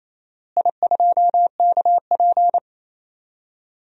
Morse code, heard as a keyed tone, about 27 wpm, 700 Hz. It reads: I2KP
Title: DXCC-100 Call Signs - 4 Characters - Repeated 3x - 28wpm